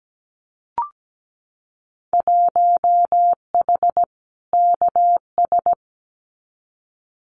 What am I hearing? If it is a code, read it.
Morse code, heard as a keyed tone, about 17 wpm, 700 Hz. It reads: 1H KS
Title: Winter Field Day Exchanges 17wpm